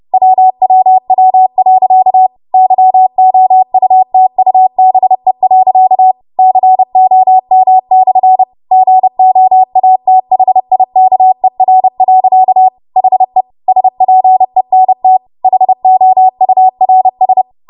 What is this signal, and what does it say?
Morse code, heard as a keyed tone, about 30 wpm, 750 Hz. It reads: WWW.YOUTUBE.COM/GOATHIKER. HE SPENT HOURS
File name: Morse / 30WPM